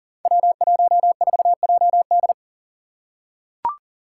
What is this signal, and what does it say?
Morse code, heard as a keyed tone, about 38 wpm, 700 Hz. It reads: W14JD
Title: Five Challenging Letter-Number Combinations 40wpm